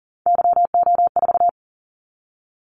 Morse code, heard as a keyed tone, about 40 wpm, 700 Hz. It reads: YO4
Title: Callsign Prefixes for Top 25 DXCC - 40wpm